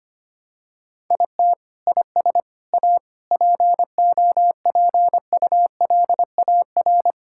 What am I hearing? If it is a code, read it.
Morse code, heard as a keyed tone, about 25 wpm, 700 Hz. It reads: IT IS A POPULAR
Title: Sentences from Top 1000 Words - Review Part 5 - 25wpm